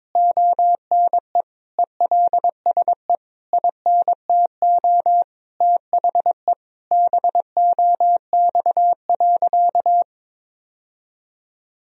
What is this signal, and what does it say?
Morse code, heard as a keyed tone, about 22 wpm, 700 Hz. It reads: ONE ELSE INTO THE BOX.
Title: Sentences from Top 1000 Words - Review Part 2 - 22wpm